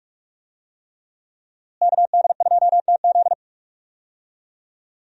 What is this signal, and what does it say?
Morse code, heard as a keyed tone, about 42 wpm, 700 Hz. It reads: KD2TZ
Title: USA Call Signs - 5 Characters - Repeated 3x - 45wpm